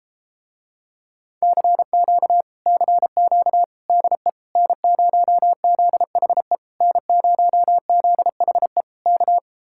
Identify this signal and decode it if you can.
Morse code, heard as a keyed tone, about 33 wpm, 700 Hz. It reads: CQ CQ DE N0ZHE N0ZHE K